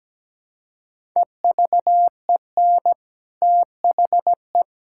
Morse code, heard as a keyed tone, about 17 wpm, 700 Hz. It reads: EVEN THE
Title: Sentences from Top 1000 Words - Review Part 6 - 17wpm